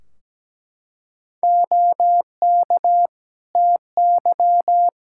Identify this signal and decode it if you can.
Morse code, heard as a keyed tone, about 17 wpm, 700 Hz. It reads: OK TY
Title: K1USN SST QSO Practice 17wpm